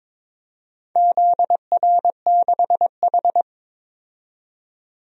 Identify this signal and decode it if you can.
Morse code, heard as a keyed tone, about 22 wpm, 700 Hz. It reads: ZR6H
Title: DXCC-100 Call Signs - 4 Characters - 22wpm